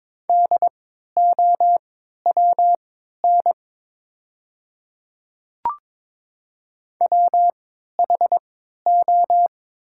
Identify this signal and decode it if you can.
Morse code, heard as a keyed tone, about 22 wpm, spaced as for 13 wpm, 700 Hz. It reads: DOWN WHO
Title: 3x Character Spacing - Top 100 Words - 22wpm